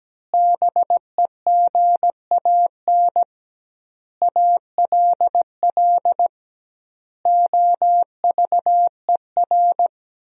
Morse code, heard as a keyed tone, about 17 wpm, 700 Hz. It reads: BEGAN ALL OVER
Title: 2x Word-Spacing - Sentences from Top 500 Words - 17wpm